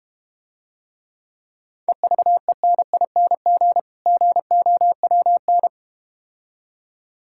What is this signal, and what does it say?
Morse code, heard as a keyed tone, about 32 wpm, 700 Hz. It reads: EVENING GOWN